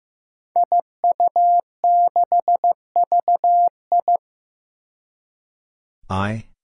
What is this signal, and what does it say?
Morse code, heard as a keyed tone, about 15 wpm, 700 Hz. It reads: IU6VI
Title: Five Challenging Letter-Number Combinations 15wpm